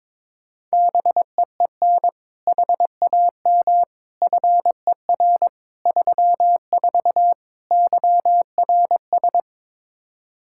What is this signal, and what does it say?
Morse code, heard as a keyed tone, about 22 wpm, 700 Hz. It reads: BEEN HAM FER 34 YRS
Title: Practice QSO - Exchange 3 - More Info - 22wpm